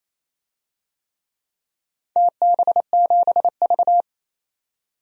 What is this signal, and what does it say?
Morse code, heard as a keyed tone, about 28 wpm, 700 Hz. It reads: TB7V
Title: DXCC-100 Call Signs - 4 Characters - 28wpm